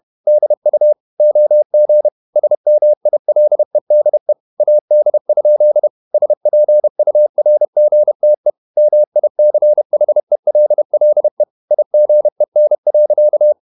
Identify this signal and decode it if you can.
Morse code, heard as a keyed tone, about 31 wpm, 600 Hz. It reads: DU OG SMILEDE AD? SPURGTE MICHELLE IGEN.
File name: Ferie QRQ 31wpm